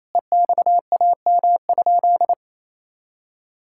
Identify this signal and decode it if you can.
Morse code, heard as a keyed tone, about 28 wpm, 700 Hz. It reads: EXAM?
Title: Sentences from Top 1000 Words - Review Part 6 - 28wpm